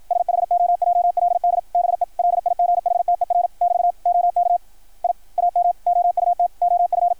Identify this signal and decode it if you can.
Morse code, heard as a keyed tone, about 53 wpm, 690 Hz. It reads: SHOJLD BE LIGHTER <BT> OK I AM OUT OF
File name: archived / CRSnet